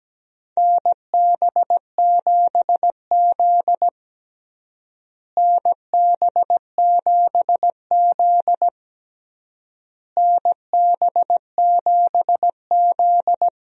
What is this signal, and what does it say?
Morse code, heard as a keyed tone, about 17 wpm, 700 Hz. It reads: NB7Z NB7Z NB7Z
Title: USA Call Signs - 4 Characters - Repeated 3x - 17wpm